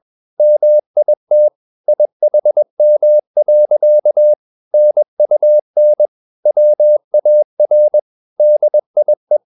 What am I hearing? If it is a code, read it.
Morse code, heard as a keyed tone, about 21 wpm, 600 Hz. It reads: MIT IHM. NUN WAR DIE